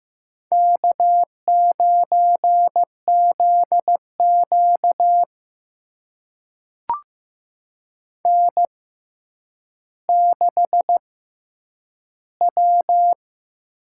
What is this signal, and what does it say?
Morse code, heard as a keyed tone, about 15 wpm, spaced as for 9 wpm, 700 Hz. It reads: K9ZQ N6W
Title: USA Call Signs - 6x Character Spacing - 15wpm